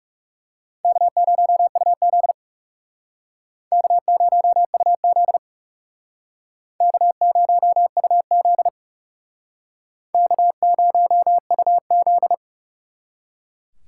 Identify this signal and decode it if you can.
Morse code, heard as a keyed tone, about 36 wpm, 700 Hz. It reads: K0UZ K0UZ K0UZ K0UZ
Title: Speed Racing - USA 1x2 Call Signs 30wpm